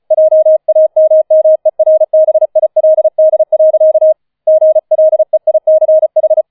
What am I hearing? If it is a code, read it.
Morse code, heard as a keyed tone, about 36 wpm, 630 Hz. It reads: JAMMERBILD. GLEICH